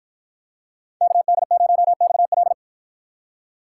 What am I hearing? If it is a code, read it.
Morse code, heard as a keyed tone, about 53 wpm, 700 Hz. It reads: KD0XL